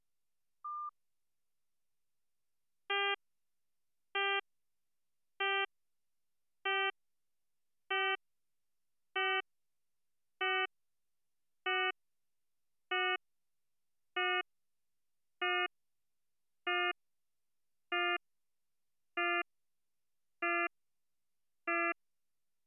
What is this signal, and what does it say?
There is no code.